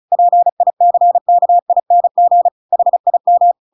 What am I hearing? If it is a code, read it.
Morse code, heard as a keyed tone, about 34 wpm, 700 Hz. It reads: PICKING HIM
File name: Pinocchio / Pinocchio ch7 35 35 ik1hge